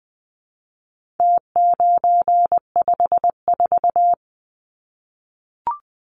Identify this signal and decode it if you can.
Morse code, heard as a keyed tone, about 20 wpm, 700 Hz. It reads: T954
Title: Numbers - Contest Style - 4-Digit 20wpm